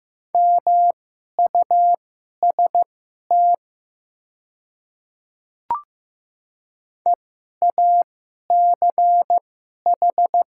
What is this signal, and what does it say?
Morse code, heard as a keyed tone, about 15 wpm, spaced as for 11 wpm, 700 Hz. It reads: MUST EACH
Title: 2x Character Spacing - Top 200 Words - 15wpm